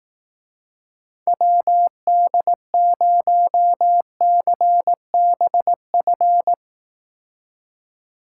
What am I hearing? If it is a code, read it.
Morse code, heard as a keyed tone, about 18 wpm, 700 Hz. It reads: WD0CBF
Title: Speed Racing - USA Call Signs 15wpm